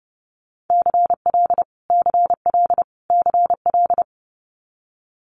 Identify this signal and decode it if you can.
Morse code, heard as a keyed tone, about 30 wpm, 700 Hz. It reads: CL CL CL
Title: Multi-letter Phonemes Repeated 3x 30wpm